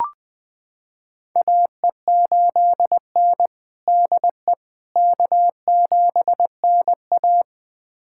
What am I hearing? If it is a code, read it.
Morse code, heard as a keyed tone, about 20 wpm, 700 Hz. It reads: AE8N DE K7NA
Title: USA Callsign DE Callsign - 4 Characters - 20wpm